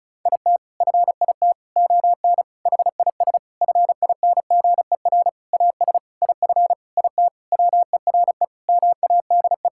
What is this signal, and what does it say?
Morse code, heard as a keyed tone, about 34 wpm, 700 Hz. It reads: IT FIT ON HIS FINGER AS IF IT WERE MADE
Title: Sentences from Top 1600 Words 35wpm